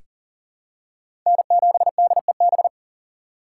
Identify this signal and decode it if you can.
Morse code, heard as a keyed tone, about 40 wpm, 700 Hz. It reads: N7DEB
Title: USA Call Signs - 5 Characters - 2x Character Spacing - 40wpm